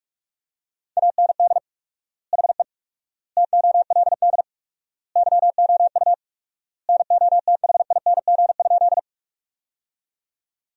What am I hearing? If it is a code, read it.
Morse code, heard as a keyed tone, about 47 wpm, 690 Hz. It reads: AND HE TOLD YOU NOTHING?
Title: 4x Word-Spacing - Sentences from Top 500 Words - 45wpm